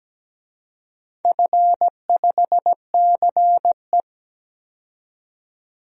Morse code, heard as a keyed tone, about 17 wpm, 700 Hz. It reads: F5CE
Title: DXCC-100 Call Signs - 4 Characters - 17wpm